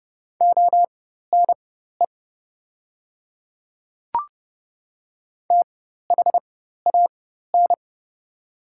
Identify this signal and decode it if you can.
Morse code, heard as a keyed tone, about 29 wpm, spaced as for 14 wpm, 700 Hz. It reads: ONE THAN
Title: 4x Character Spacing - Top 100 Words - 30wpm